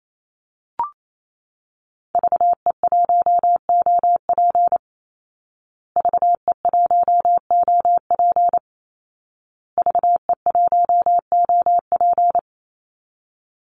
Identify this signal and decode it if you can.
Morse code, heard as a keyed tone, about 28 wpm, 700 Hz. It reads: VE1OP VE1OP VE1OP
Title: Call Signs Repeated 2x - 28wpm